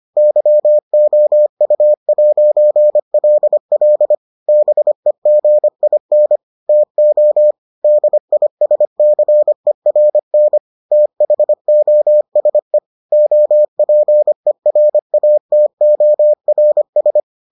Morse code, heard as a keyed tone, about 25 wpm, 600 Hz. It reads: YOU'LL BEGIN TO DISCERN THOSE OPERATORS